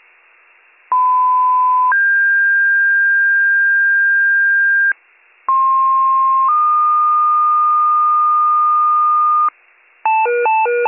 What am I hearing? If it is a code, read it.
Two-tone page, heard as a one-second tone, then a three-second tone, about 1010 then 1600 Hz, about 1060 then 1220 Hz, with noise. It is heard from 0.9 to 9.5 s.